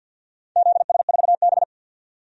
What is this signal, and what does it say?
Morse code, heard as a keyed tone, about 53 wpm, 700 Hz. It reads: ZS4B
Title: DXCC-100 Call Signs - 4 Characters - 50wpm